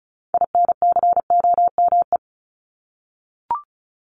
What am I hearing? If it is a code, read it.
Morse code, heard as a keyed tone, about 34 wpm, 700 Hz. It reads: INCOME
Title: Top 751-1000 Words 35wpm